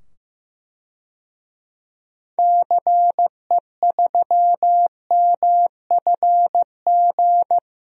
Morse code, heard as a keyed tone, about 16 wpm, 700 Hz. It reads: CE3MFG
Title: DXCC-100 Call Signs - 6 Characters - 15wpm